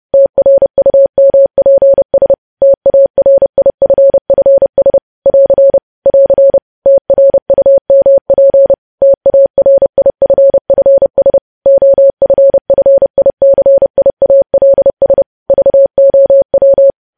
Morse code, heard as a keyed tone, about 30 wpm, 550 Hz. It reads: TRUMPS TARIFFS <AR> <AR> TRUMP TARIFFS OFFICIALS VOW
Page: news headlines in morse code at 30 wpm